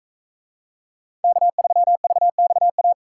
Morse code, heard as a keyed tone, about 44 wpm, 700 Hz. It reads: K3VXA